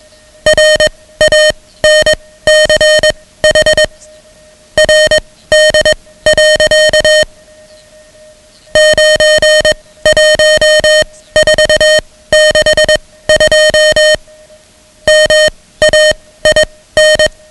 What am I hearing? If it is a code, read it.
Morse code, heard as a keyed tone, about 21 wpm, spaced as for 15 wpm, 620 Hz, with noise. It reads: RANCH RD. 91462 MAIN
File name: ham radio / CodeQuick / Code Quick 10